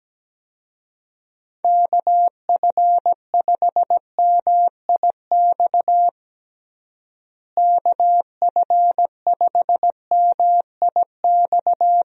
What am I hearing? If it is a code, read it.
Morse code, heard as a keyed tone, about 17 wpm, 700 Hz. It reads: KF5MIX KF5MIX
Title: USA Call Signs - 6 Characters - Repeated 3x - 17wpm